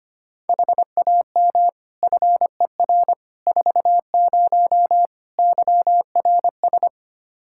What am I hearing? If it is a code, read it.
Morse code, heard as a keyed tone, about 25 wpm, 700 Hz. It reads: HAM FER 40 YRS
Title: Practice QSO - Exchange 3 - More Info - 25wpm